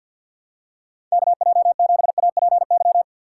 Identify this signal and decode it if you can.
Morse code, heard as a keyed tone, about 53 wpm, 700 Hz. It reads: KJ7APY
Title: USA Call Signs - 6 Characters - 4x Character Spacing - 50wpm